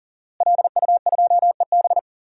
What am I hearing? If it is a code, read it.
Morse code, heard as a keyed tone, about 39 wpm, 700 Hz. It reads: LU2EB
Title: DXCC-100 Call Signs - 5 Characters - Repeated 3x - 40wpm